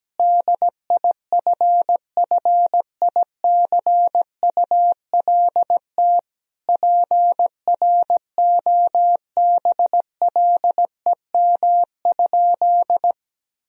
Morse code, heard as a keyed tone, about 17 wpm, 700 Hz. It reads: DIFFICULT PROBLEM?
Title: Sentences from Top 1000 Words - Review Part 2 - 17wpm